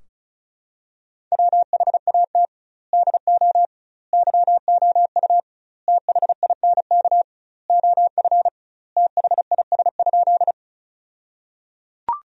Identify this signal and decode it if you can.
Morse code, heard as a keyed tone, about 36 wpm, 700 Hz. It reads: WHAT DO YOU THINK OF THIS?
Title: 2x Word-Spacing - Sentences from Top 200 Words - 35wpm